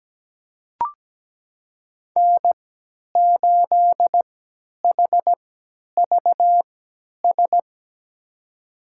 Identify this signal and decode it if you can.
Morse code, heard as a keyed tone, about 17 wpm, spaced as for 10 wpm, 700 Hz. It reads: N8HVS
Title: USA Call Signs - 5 Characters - 3x Character Spacing - 17wpm